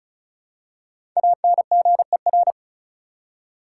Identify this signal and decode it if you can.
Morse code, heard as a keyed tone, about 36 wpm, 700 Hz. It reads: ANGER
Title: Five Letter Words - 35wpm